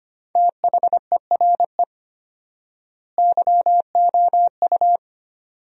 Morse code, heard as a keyed tone, about 25 wpm, 700 Hz. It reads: THERE YOU
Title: 4x Word-Spacing - Sentences from Top 100 Words - 25wpm